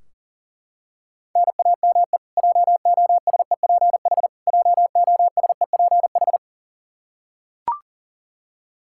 Morse code, heard as a keyed tone, about 40 wpm, 700 Hz. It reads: NAME JOSEPH JOSEPH
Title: K1USN SST QSO Practice 40wpm